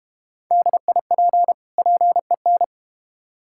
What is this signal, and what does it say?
Morse code, heard as a keyed tone, about 32 wpm, 700 Hz. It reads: DIP PEN